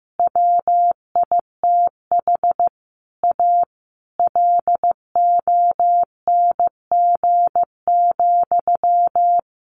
Morse code, heard as a keyed tone, about 15 wpm, 700 Hz. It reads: WITH A LONG,
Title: Sentences from Top 1200 Words 15wpm